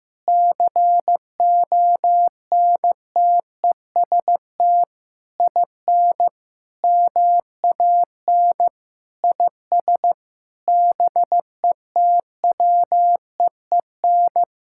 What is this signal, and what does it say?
Morse code, heard as a keyed tone, about 16 wpm, 700 Hz. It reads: CONTEST IN MAN IS BETWEEN
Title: Sentences from Top 1600 Words 15wpm